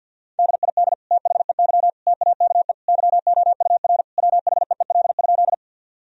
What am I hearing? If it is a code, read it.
Morse code, heard as a keyed tone, about 50 wpm, 700 Hz. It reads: DID THEY TAKE YOUR WHEEL?
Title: Sentences from Top 1000 Words - Review Part 3 - 50wpm